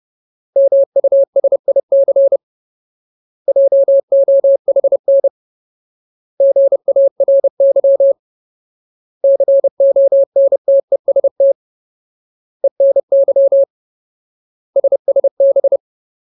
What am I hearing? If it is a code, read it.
Morse code, heard as a keyed tone, about 30 wpm, 550 Hz. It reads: MUSIC JOHN GARY CONTEST ENY SSB
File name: Ultimate Word List / 30WPM Ultimate